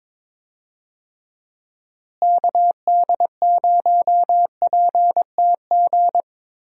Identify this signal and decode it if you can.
Morse code, heard as a keyed tone, about 22 wpm, 700 Hz. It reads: KD0PTG
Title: USA Call Signs - 6 Characters - Repeated 3x - 22wpm